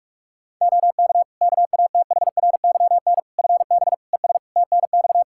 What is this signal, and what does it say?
Morse code, heard as a keyed tone, about 46 wpm, 700 Hz. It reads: OK KATHRYN FB ES TNX
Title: Practice QSO - Exchange 2 - Info - 45wpm